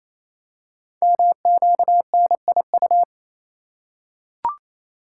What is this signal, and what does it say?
Morse code, heard as a keyed tone, about 27 wpm, 700 Hz. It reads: MQNIU
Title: Code Groups - 5 Letters - 28wpm